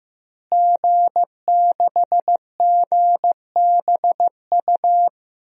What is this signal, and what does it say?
Morse code, heard as a keyed tone, about 15 wpm, 700 Hz. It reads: G6GBU
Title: DXCC-100 Call Signs - 5 Characters - Repeated 3x - 15wpm